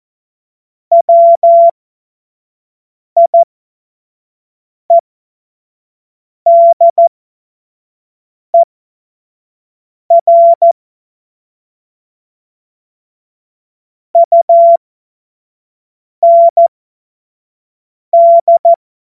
Morse code, heard as a keyed tone, about 14 wpm, spaced as for 5 wpm, 680 Hz. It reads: WIEDER UND